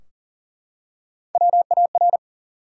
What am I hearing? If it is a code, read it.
Morse code, heard as a keyed tone, about 42 wpm, 700 Hz. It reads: WAR
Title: Three Letter Words - 40wpm